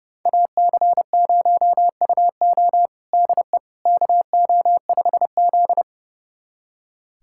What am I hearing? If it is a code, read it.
Morse code, heard as a keyed tone, about 32 wpm, 700 Hz. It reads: AC0UO DE KO5Z
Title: USA Callsign DE Callsign 30wpm